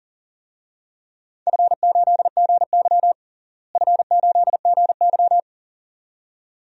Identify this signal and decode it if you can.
Morse code, heard as a keyed tone, about 40 wpm, 700 Hz. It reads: F8GY F8GY
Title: DXCC-100 Call Signs - 4 Characters - Repeated 2x - 40wpm